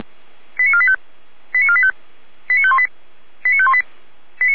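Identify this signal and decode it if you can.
Five-tone selective call (ZVEI), heard as quick runs of five tones, tones of about 70 ms. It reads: 89485 89485 89518 89518
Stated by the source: ZVEI five-tone selective call received on an SDR.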